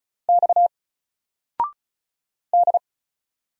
Morse code, heard as a keyed tone, about 34 wpm, 700 Hz. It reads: X D
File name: Single-Letter-Number-35wpm